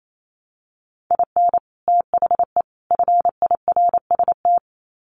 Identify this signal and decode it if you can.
Morse code, heard as a keyed tone, about 28 wpm, 700 Hz. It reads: IN THE FIRST
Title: Sets of 3 Words 28wpm